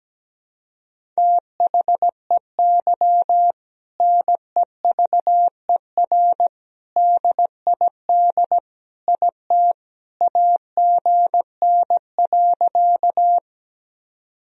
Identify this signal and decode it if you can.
Morse code, heard as a keyed tone, about 17 wpm, 700 Hz. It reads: THEY NEVER DID IT AGN.
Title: Sentences from Top 300 Words and Abbreviations 17wpm